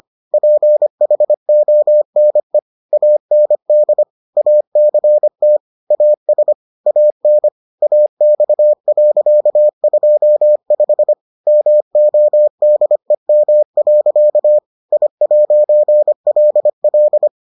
Morse code, heard as a keyed tone, about 25 wpm, 600 Hz. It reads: PHONE AND ACT AS AN AX.25 MODEM. I'LL